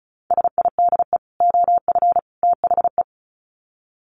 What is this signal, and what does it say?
Morse code, heard as a keyed tone, about 34 wpm, 700 Hz. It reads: SIDE OF THE
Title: Sets of 3 Words 35wpm